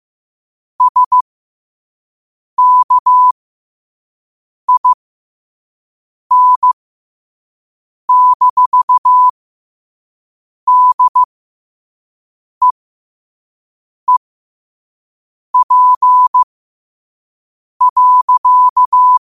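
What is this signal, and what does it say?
Morse code, heard as a keyed tone, about 15 wpm, spaced as for 5 wpm, 1000 Hz. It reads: SKIN-DEEP.
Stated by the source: Morse code sent at 5 wpm.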